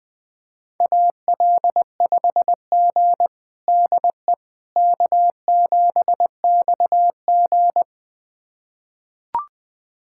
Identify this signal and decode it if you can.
Morse code, heard as a keyed tone, about 20 wpm, 700 Hz. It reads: AL5G DE K7XG
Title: USA Callsign DE Callsign - 4 Characters - 20wpm